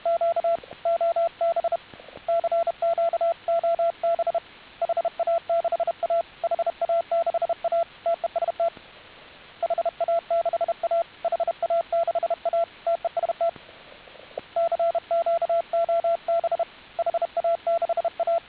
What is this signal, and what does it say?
Morse code, heard as a keyed tone, about 31 wpm, 690 Hz, with radio noise. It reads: Q OB CQ OB HA6A HA6A TEST HA6A HA6A TEST CQ OB HA6A